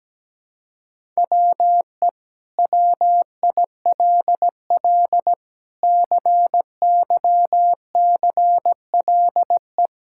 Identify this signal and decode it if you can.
Morse code, heard as a keyed tone, about 17 wpm, 700 Hz. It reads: WE WILL CYCLE
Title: Sentences from Top 1000 Words - Review Part 6 - 17wpm